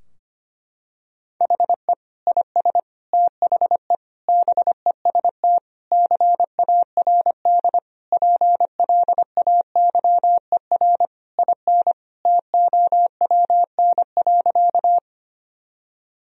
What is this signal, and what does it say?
Morse code, heard as a keyed tone, about 25 wpm, 700 Hz. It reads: HE IS THE BEST CARD PLAYER IN TOWN.
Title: Sentences from Top 1000 Words - Review Part 4 - 25wpm